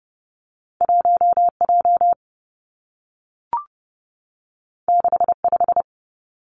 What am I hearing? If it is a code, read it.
Morse code, heard as a keyed tone, about 30 wpm, 700 Hz. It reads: 1 J 6 5
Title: Two Challenging Letter-Number Combinations 30wpm